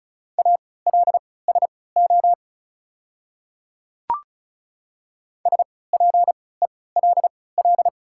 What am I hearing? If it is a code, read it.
Morse code, heard as a keyed tone, about 34 wpm, spaced as for 20 wpm, 690 Hz. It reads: ALSO SPELL
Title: 3x Character Spacing - Top 200 Words - 35wpm